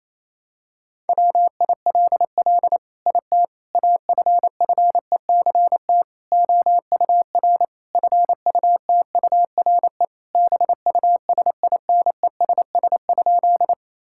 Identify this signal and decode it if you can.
Morse code, heard as a keyed tone, about 28 wpm, 700 Hz. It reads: WILL IT AFFECT OUR FUTURE BUSINESS?
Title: Sentences from Top 1000 Words - Review Part 6 - 28wpm